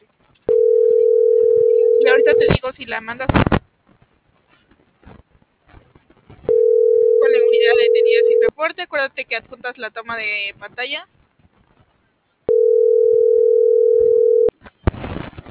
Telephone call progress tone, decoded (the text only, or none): ringback tone